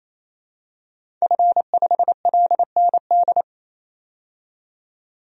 Morse code, heard as a keyed tone, about 28 wpm, 700 Hz. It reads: F5LND
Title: DXCC-100 Call Signs - 5 Characters - 28wpm